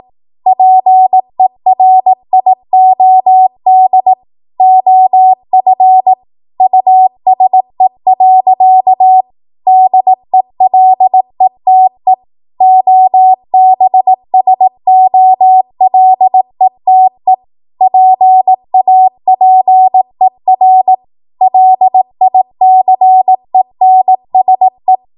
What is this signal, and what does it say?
Morse code, heard as a keyed tone, about 18 wpm, 750 Hz. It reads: PERIOD OF USE. DELETE OBSOLETE PAPER LICENSE
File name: Morse / 18WPM